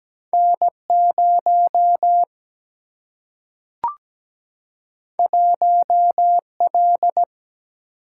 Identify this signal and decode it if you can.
Morse code, heard as a keyed tone, about 17 wpm, 700 Hz. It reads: N0 1L
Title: Single-Letter Single-Number 17wpm